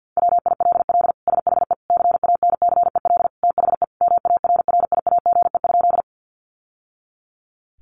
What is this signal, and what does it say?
Morse code, heard as a keyed tone, about 48 wpm, 700 Hz. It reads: WILL SHE CANCEL THE MARRIAGE?
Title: Sentences from Top 1200 Words 50wpm